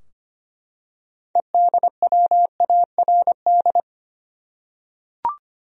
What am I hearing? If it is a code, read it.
Morse code, heard as a keyed tone, about 25 wpm, 700 Hz. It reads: EDWARD
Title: USA Common Names - 25wpm